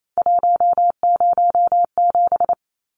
Morse code, heard as a keyed tone, about 28 wpm, 700 Hz. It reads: 107
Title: Numbers - 3-Digit 28wpm